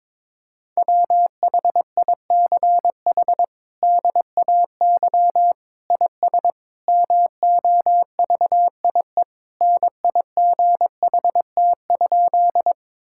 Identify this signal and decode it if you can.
Morse code, heard as a keyed tone, about 21 wpm, 700 Hz. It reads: WHICH DAY IS MOVIE NIGHT?
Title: Sentences from Top 1000 Words - Review Part 5 - 22wpm